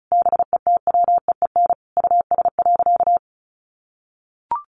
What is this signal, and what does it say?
Morse code, heard as a keyed tone, about 34 wpm, 700 Hz. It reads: BETWEEN US.
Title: Sentences from Top 800 Words 35wpm